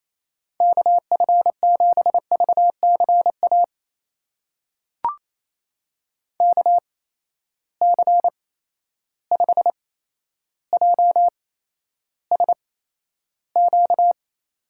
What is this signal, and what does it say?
Morse code, heard as a keyed tone, about 28 wpm, spaced as for 12 wpm, 700 Hz. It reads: KF7VCA KC5JSQ
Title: USA Call Signs - 8x Character Spacing - 28wpm